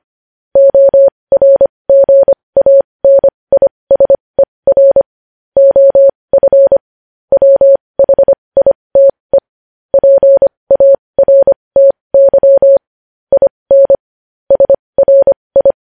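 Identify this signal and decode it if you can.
Morse code, heard as a keyed tone, about 25 wpm, spaced as for 20 wpm, 550 Hz. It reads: ORGANISER OF WHITE PARTY IN SRI
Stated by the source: Morse code news headlines sent at 20 wpm.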